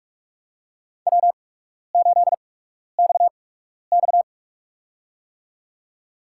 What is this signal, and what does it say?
Morse code, heard as a keyed tone, about 45 wpm, spaced as for 12 wpm, 690 Hz. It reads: W8XX
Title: USA Call Signs - 8x Character Spacing - 45wpm